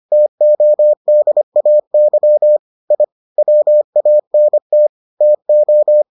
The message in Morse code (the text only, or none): TODAY I WANT TO